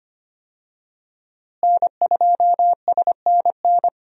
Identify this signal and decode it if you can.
Morse code, heard as a keyed tone, about 25 wpm, 700 Hz. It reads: N2SNN
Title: USA Call Signs - 5 Characters - Repeated 3x - 25wpm